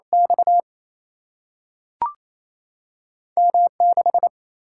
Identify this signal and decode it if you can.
Morse code, heard as a keyed tone, about 28 wpm, 700 Hz. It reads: X M6
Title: Single-Letter Single-Number 28wpm